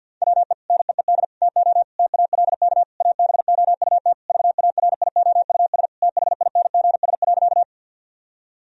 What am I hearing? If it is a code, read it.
Morse code, heard as a keyed tone, about 50 wpm, 700 Hz. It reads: WE NEED TO TALK ABOUT VARIOUS THINGS.
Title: Sentences from Top 1000 Words - Review Part 4 - 50wpm